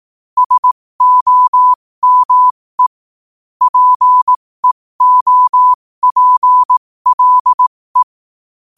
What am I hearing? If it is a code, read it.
Morse code, heard as a keyed tone, about 18 wpm, spaced as for 15 wpm, 1000 Hz. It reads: SOME PEOPLE
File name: qotdfiles / 15wpm